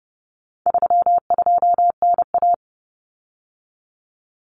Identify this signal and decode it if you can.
Morse code, heard as a keyed tone, about 30 wpm, 700 Hz. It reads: 32NA
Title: Common Cut Numbers - 4-Digit 30wpm